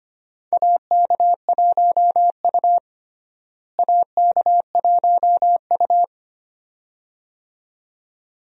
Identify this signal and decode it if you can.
Morse code, heard as a keyed tone, about 25 wpm, 700 Hz. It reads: AK1U AK1U
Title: USA Call Signs - 4 Characters - Repeated 2x - 25wpm